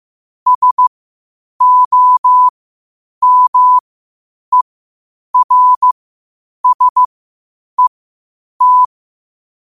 Morse code, heard as a keyed tone, about 15 wpm, spaced as for 8 wpm, 1000 Hz. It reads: SOMERSET
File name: qotdfiles / 8wpm